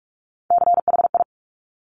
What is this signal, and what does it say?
Morse code, heard as a keyed tone, about 46 wpm, 690 Hz. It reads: CHI
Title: Top Three Letter Combinations in Words 45wpm